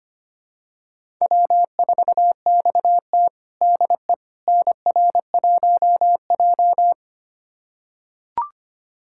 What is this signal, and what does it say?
Morse code, heard as a keyed tone, about 25 wpm, 700 Hz. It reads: W4XT DE NR1J
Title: USA Callsign DE Callsign - 4 Characters - 25wpm